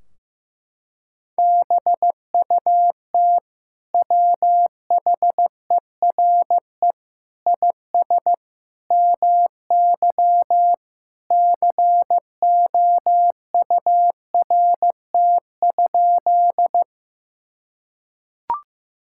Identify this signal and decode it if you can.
Morse code, heard as a keyed tone, about 14 wpm, 700 Hz. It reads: BUT WHERE IS MY COURT?
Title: Sentences from Top 1600 Words 15wpm